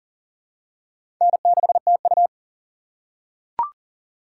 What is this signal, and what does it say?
Morse code, heard as a keyed tone, about 40 wpm, 700 Hz. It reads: N6TU
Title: USA Call Signs - 4 Characters - 6x Character Spacing - 40wpm